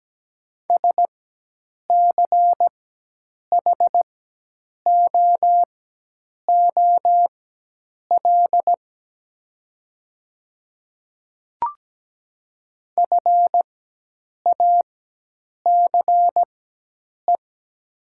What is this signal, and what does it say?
Morse code, heard as a keyed tone, about 17 wpm, spaced as for 8 wpm, 700 Hz. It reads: SCHOOL FACE
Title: 4x Character Spacing - Top 500 Words - 17wpm